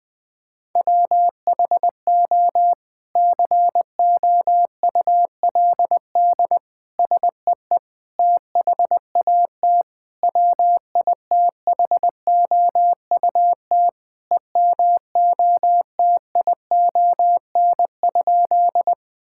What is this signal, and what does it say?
Morse code, heard as a keyed tone, about 20 wpm, 700 Hz. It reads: WHO COULD SEE THAT WITHOUT EMOTION?
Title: Sentences from Top 1600 Words 20wpm